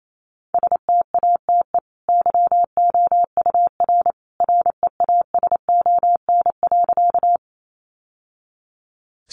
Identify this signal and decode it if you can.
Morse code, heard as a keyed tone, about 28 wpm, 700 Hz. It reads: STATE YOUR REASON.
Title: Sentences from Top 700 Words 28wpm